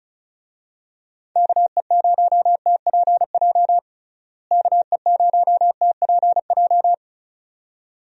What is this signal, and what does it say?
Morse code, heard as a keyed tone, about 34 wpm, 700 Hz. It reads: KE0TPJ KE0TPJ
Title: USA Call Signs - 6 Characters - Repeated 2x - 35wpm